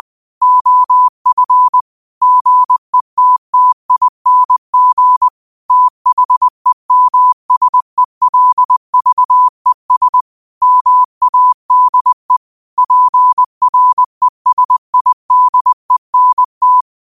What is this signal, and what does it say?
Morse code, heard as a keyed tone, about 20 wpm, 1000 Hz. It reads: OF GETTING THEMSELVES MADE PRESIDENT
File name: qotdfiles / 20wpm